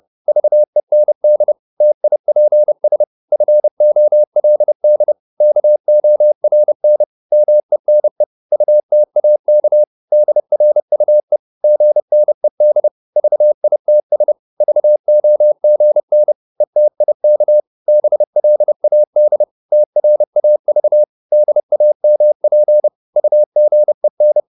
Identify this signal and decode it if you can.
Morse code, heard as a keyed tone, about 30 wpm, 600 Hz. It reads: VEND TIPS FOLD KORN MENE UTAK DRUE GNED VITS VOGN ETIK BLAD TRAV DAMP UGEN
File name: Korte ord DK 30wpm